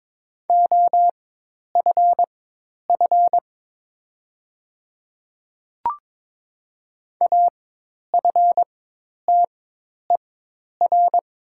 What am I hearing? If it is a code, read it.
Morse code, heard as a keyed tone, about 22 wpm, spaced as for 10 wpm, 700 Hz. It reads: OFF AFTER
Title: 4x Character Spacing - Top 200 Words - 22wpm